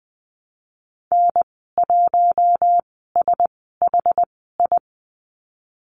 Morse code, heard as a keyed tone, about 20 wpm, spaced as for 15 wpm, 700 Hz. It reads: N1SHI
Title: 076 Lesson 13g - Letter L - Call Signs 20wpm